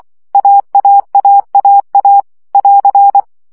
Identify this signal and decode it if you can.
Morse code, heard as a keyed tone, about 24 wpm, 800 Hz. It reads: AAAAA <AR>
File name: CW gyakorlas a 20 baud